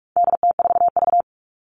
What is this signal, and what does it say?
Morse code, heard as a keyed tone, about 44 wpm, 700 Hz. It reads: DT4V
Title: All Cut Numbers - 4-Digit 45wpm